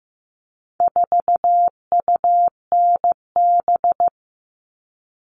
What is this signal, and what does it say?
Morse code, heard as a keyed tone, about 15 wpm, 700 Hz. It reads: 4UNB
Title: All Cut Numbers - 4-Digit 15wpm